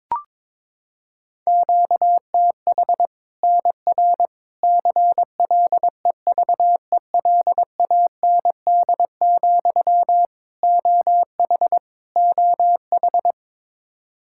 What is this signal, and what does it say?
Morse code, heard as a keyed tone, about 21 wpm, 700 Hz. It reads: QTH NR CLEVELAND, OH OH
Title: Practice QSO - Exchange 1 - Beginning - 22wpm